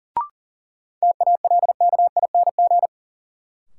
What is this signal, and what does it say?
Morse code, heard as a keyed tone, about 37 wpm, 700 Hz. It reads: TALKING
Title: Sets of 2 Words - New Word Drill - 40wpm